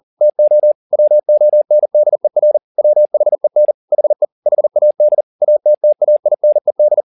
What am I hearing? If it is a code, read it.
Morse code, heard as a keyed tone, about 40 wpm, 600 Hz. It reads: TO WONDER WHEN HE HAD ATTAINED